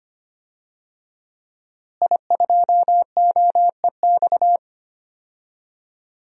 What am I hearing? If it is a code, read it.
Morse code, heard as a keyed tone, about 25 wpm, 700 Hz. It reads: I2OEX
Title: DXCC-100 Call Signs - 5 Characters - Repeated 2x - 25wpm